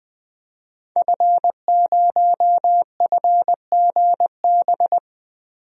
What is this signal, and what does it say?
Morse code, heard as a keyed tone, about 20 wpm, 700 Hz. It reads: F0FGB
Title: DXCC-100 Call Signs - 5 Characters - Repeated 3x - 20wpm